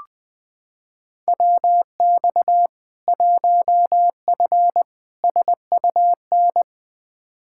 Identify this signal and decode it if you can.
Morse code, heard as a keyed tone, about 20 wpm, 700 Hz. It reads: WX 1F SUN
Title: Practice QSO - Exchange 2 - Info - 20wpm